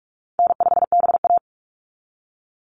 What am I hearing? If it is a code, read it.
Morse code, heard as a keyed tone, about 46 wpm, 700 Hz. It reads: N5BA
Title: Easy Call Signs - 45wpm